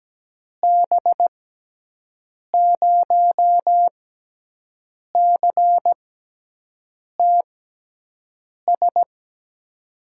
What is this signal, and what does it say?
Morse code, heard as a keyed tone, about 17 wpm, spaced as for 6 wpm, 700 Hz. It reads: B0CTS
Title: USA Call Signs - 6 Characters - 6x Character Spacing - 17wpm